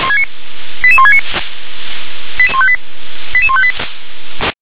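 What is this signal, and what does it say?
Five-tone selective call (ZVEI), heard as quick runs of five tones, tones of about 70 ms. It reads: 88169 88158
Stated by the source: ZVEI five-tone selective call received on an SDR.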